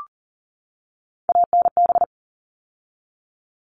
Morse code, heard as a keyed tone, about 40 wpm, 700 Hz. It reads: ANB
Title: All Cut Numbers - 3-Digit 40wpm